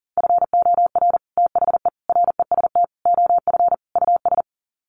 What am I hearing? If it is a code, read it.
Morse code, heard as a keyed tone, about 37 wpm, 700 Hz. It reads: FOR THE REST OF US
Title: Sets of 5 Words Encore 40wpm